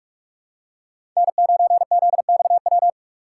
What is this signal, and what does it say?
Morse code, heard as a keyed tone, about 46 wpm, 700 Hz. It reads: N9ZXW
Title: USA Call Signs - 5 Characters - 2x Character Spacing - 45wpm